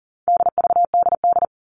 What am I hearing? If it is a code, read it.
Morse code, heard as a keyed tone, about 40 wpm, 700 Hz. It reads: DVDD
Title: All Cut Numbers - 4-Digit 40wpm